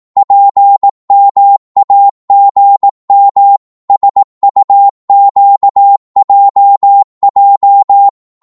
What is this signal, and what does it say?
Morse code, heard as a keyed tone, about 18 wpm, 800 Hz. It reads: PMAGM SUQJJ
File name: Morse / ltr-18wpm